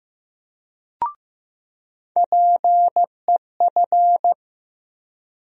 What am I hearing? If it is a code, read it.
Morse code, heard as a keyed tone, about 15 wpm, 700 Hz. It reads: PEF
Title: USA Call Sign Postfixes - 3 Letters 15wpm